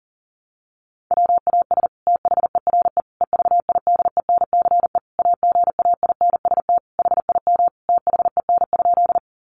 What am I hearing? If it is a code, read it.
Morse code, heard as a keyed tone, about 37 wpm, 700 Hz. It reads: WAS THERE EVIDENCE AGAINST HIM THEN?
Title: Sentences from Top 1300 Words 40wpm